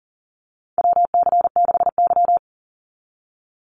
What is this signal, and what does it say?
Morse code, heard as a keyed tone, about 40 wpm, 700 Hz. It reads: WC6Y
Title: Call Signs Repeated 2x - 40wpm